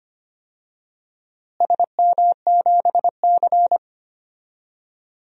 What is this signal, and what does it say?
Morse code, heard as a keyed tone, about 25 wpm, 700 Hz. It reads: SM7C
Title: DXCC-100 Call Signs - 4 Characters - Repeated 3x - 25wpm